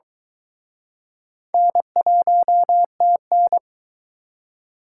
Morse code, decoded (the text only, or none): N1TN